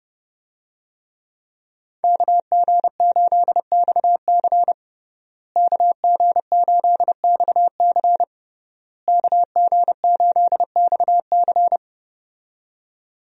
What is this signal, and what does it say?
Morse code, heard as a keyed tone, about 28 wpm, 700 Hz. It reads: KG8XC KG8XC KG8XC
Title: USA Call Signs - 5 Characters - Repeated 3x - 30wpm